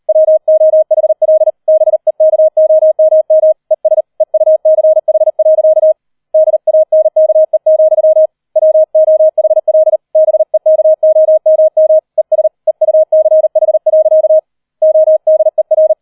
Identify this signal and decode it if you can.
Morse code, heard as a keyed tone, about 38 wpm, 630 Hz. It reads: WOHL BEKOMM ES EUCH. DANKE, WOHL BEKOMM ES EUCH. ODER
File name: 40wpm